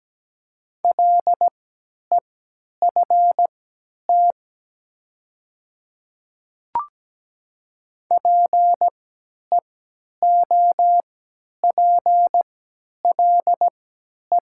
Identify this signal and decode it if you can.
Morse code, heard as a keyed tone, about 17 wpm, spaced as for 10 wpm, 700 Hz. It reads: LEFT PEOPLE
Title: 3x Character Spacing - Top 300 Words - 17wpm